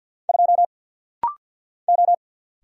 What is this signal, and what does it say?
Morse code, heard as a keyed tone, about 50 wpm, 700 Hz. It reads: 2 O
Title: Single Letter-Number - Rapid-Fire 50wpm